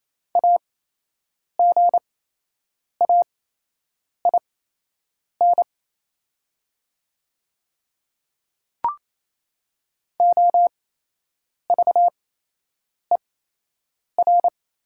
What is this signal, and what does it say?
Morse code, heard as a keyed tone, about 28 wpm, spaced as for 8 wpm, 700 Hz. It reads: AGAIN OVER
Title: 8x Character Spacing - Top 500 Words - 28wpm